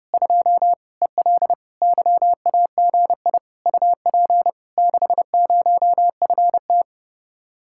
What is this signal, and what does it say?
Morse code, heard as a keyed tone, about 31 wpm, 700 Hz. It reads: 2 EL YAGI UP 60FT